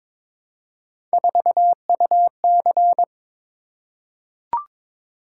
Morse code, heard as a keyed tone, about 22 wpm, 700 Hz. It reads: 4UC
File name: practice / 22wpm / Three Letter-Number Combinations - 22wpm